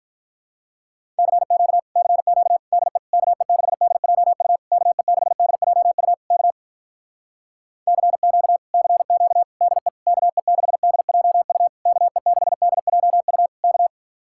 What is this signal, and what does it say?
Morse code, heard as a keyed tone, about 48 wpm, 700 Hz. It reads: CQ CQ DE KE6DJU KE6DJU K CQ CQ DE KE6DJU KE6DJU K